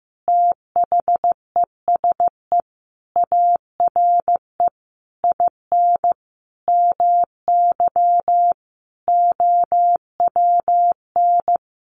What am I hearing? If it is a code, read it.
Morse code, heard as a keyed tone, about 15 wpm, 700 Hz. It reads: THESE ARE IN MY OWN
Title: Sentences from Top 1200 Words 15wpm